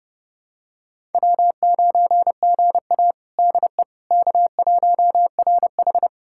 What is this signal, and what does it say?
Morse code, heard as a keyed tone, about 30 wpm, 700 Hz. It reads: W9GA DE K1RH
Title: USA Callsign DE Callsign - 4 Characters - 30wpm